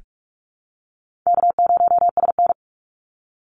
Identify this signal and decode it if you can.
Morse code, heard as a keyed tone, about 44 wpm, 700 Hz. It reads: K0SN
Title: Call Signs Repeated 2x - 45wpm